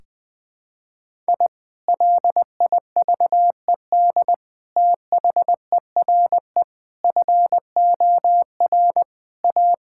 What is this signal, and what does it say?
Morse code, heard as a keyed tone, about 20 wpm, 700 Hz. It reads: I LIVED THERE FOR A
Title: Sentences with 3-Word Phrases - 20wpm